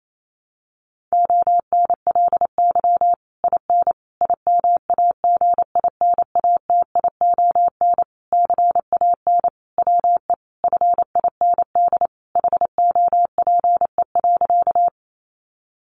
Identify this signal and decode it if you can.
Morse code, heard as a keyed tone, about 28 wpm, 700 Hz. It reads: ONLY IN IMAGINATION CAN WE FIND HOPE.
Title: Sentences from Top 1300 Words 28wpm